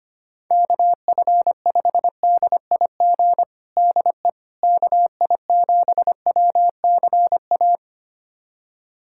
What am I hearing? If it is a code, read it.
Morse code, heard as a keyed tone, about 25 wpm, 700 Hz. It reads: KF5DIG DE KI7WCA
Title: USA Callsign DE Callsign 25wpm